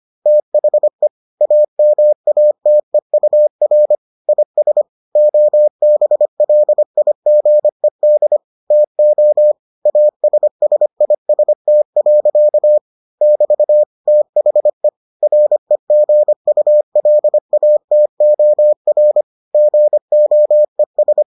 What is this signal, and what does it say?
Morse code, heard as a keyed tone, about 25 wpm, 600 Hz. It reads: THE AMATEUR IS OBLIGED TO ASSIST. <BT> THE REGULATOR GOES